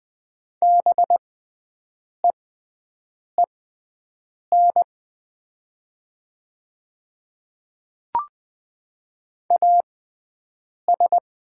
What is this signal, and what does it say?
Morse code, heard as a keyed tone, about 20 wpm, spaced as for 7 wpm, 700 Hz. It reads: BEEN AS
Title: 6x Character Spacing - Top 100 Words - 20wpm